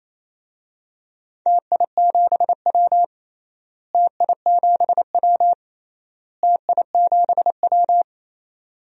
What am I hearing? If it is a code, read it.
Morse code, heard as a keyed tone, about 27 wpm, 700 Hz. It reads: TI7W TI7W TI7W
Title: DXCC-100 Call Signs - 4 Characters - Repeated 3x - 28wpm